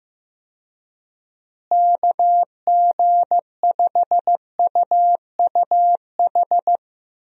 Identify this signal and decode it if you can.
Morse code, heard as a keyed tone, about 15 wpm, 700 Hz. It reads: KG5UUH
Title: USA Call Signs - 6 Characters - Repeated 3x - 15wpm